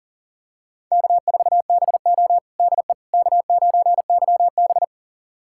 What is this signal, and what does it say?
Morse code, heard as a keyed tone, about 40 wpm, 700 Hz. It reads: K4BO DE K9YB
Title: USA Callsign DE Callsign - 4 Characters - 40wpm